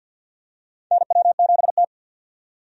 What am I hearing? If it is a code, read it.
Morse code, heard as a keyed tone, about 51 wpm, 700 Hz. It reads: NW7T
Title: USA Call Signs - 4 Characters - 50wpm